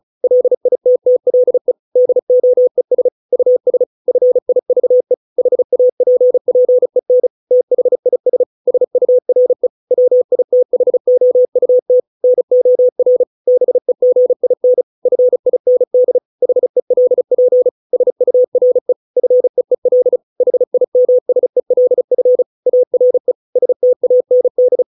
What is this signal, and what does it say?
Morse code, heard as a keyed tone, about 34 wpm, 500 Hz. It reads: LITTLE DOES US FIVE HAPPEN THIS SURE WITHOUT NOR BEGIN FIND HELP SURE FEEL HIMSELF ARE STAND